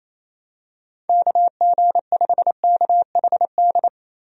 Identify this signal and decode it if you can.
Morse code, heard as a keyed tone, about 28 wpm, 700 Hz. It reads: KG5KHD
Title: USA Call Signs - 6 Characters - Repeated 3x - 28wpm